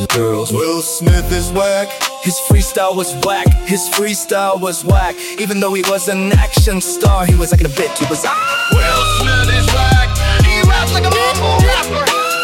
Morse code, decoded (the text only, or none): none